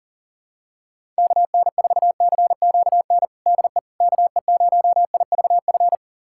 Morse code, heard as a keyed tone, about 40 wpm, 700 Hz. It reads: KN4CQN DE KE0IVF
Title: USA Callsign DE Callsign - 6 Characters - 40wpm